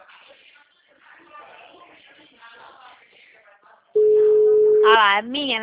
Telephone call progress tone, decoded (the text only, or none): ringback tone